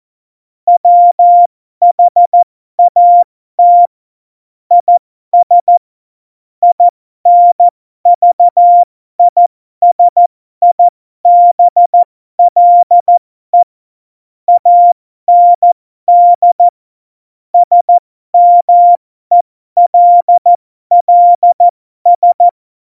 Morse code, heard as a keyed tone, about 14 wpm, spaced as for 12 wpm, 700 Hz. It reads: WHAT IS INVISIBLE AND SMELLS
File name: Morsekurs / Witze